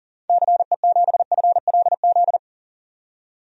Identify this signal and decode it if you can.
Morse code, heard as a keyed tone, about 38 wpm, 700 Hz. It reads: CE7FLZ
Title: DXCC-100 Call Signs - 6 Characters - 40wpm